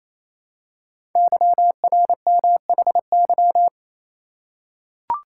Morse code, heard as a keyed tone, about 29 wpm, 700 Hz. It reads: YRMHY
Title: Code Groups - 5 Letters - 28wpm